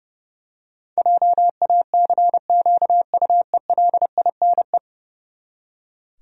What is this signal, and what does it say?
Morse code, heard as a keyed tone, about 30 wpm, 700 Hz. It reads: JACQUELINE
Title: USA Common Names - 30wpm